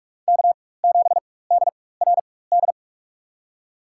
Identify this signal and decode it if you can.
Morse code, heard as a keyed tone, about 46 wpm, spaced as for 21 wpm, 700 Hz. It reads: K7DRD
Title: USA Call Signs - 5 Characters - 4x Character Spacing - 45wpm